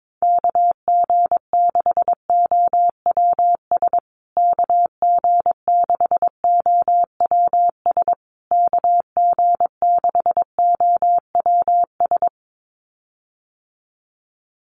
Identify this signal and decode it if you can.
Morse code, heard as a keyed tone, about 22 wpm, 700 Hz. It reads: KG6OWS KG6OWS KG6OWS
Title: USA Call Signs Repeated 3x 22wpm